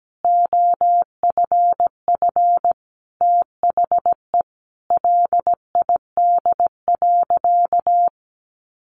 Morse code, heard as a keyed tone, about 17 wpm, 700 Hz. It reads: OFF THE LID.
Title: Easy Phrases 17wpm